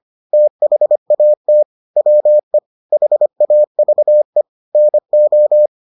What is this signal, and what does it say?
Morse code, heard as a keyed tone, about 25 wpm, 600 Hz. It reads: THAT WE HAVE NO